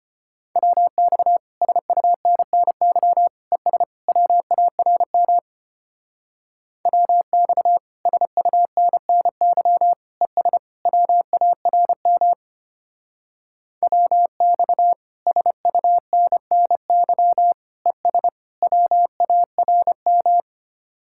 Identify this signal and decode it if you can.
Morse code, heard as a keyed tone, about 30 wpm, 700 Hz. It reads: WX SUNNY ES WARM WX SUNNY ES WARM WX SUNNY ES WARM
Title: Speed-Racing Common QSO Phrases - 25wpm